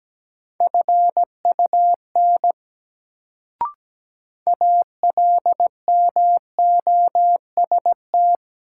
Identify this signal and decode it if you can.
Morse code, heard as a keyed tone, about 17 wpm, 700 Hz. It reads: FUN ALMOST
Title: Sentences from Top 700 Words - New Word Drill - 17wpm